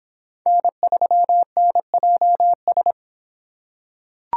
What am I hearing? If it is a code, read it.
Morse code, heard as a keyed tone, about 26 wpm, 700 Hz. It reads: N3NJS